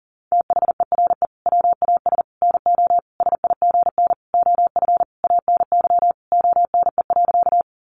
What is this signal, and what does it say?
Morse code, heard as a keyed tone, about 40 wpm, 700 Hz. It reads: THERE WAS NO SIGN OF ANY ONE.
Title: Sentences from Top 1000 Words 40wpm